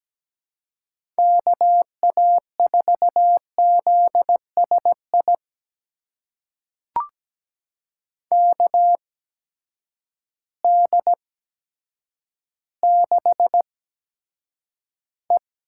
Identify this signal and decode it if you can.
Morse code, heard as a keyed tone, about 17 wpm, spaced as for 8 wpm, 700 Hz. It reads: KA4ZSI KD6E
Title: USA Call Signs - 6 Characters - 8x Character Spacing - 17wpm